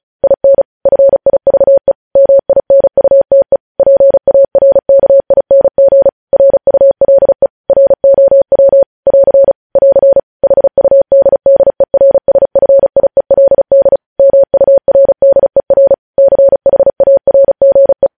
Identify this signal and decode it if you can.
Morse code, heard as a keyed tone, about 36 wpm, 550 Hz. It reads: IN FIVE MINUTE PARKING RULE ROW <AR> <AR> HUDDERSFIELD MURDER CHARGE
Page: news headlines in morse code at 35 wpm